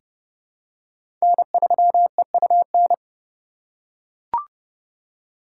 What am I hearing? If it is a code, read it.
Morse code, heard as a keyed tone, about 30 wpm, 700 Hz. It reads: N3EUN
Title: USA Call Signs - 5 Characters - 8x Character Spacing - 30wpm